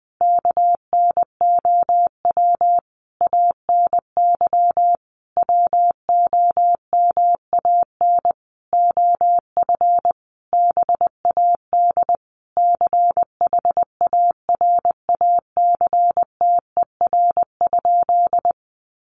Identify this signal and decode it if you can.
Morse code, heard as a keyed tone, about 20 wpm, 700 Hz. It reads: KNOW ANY WOMAN OF BAD CHARACTER?
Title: Sentences from Top 1200 Words 20wpm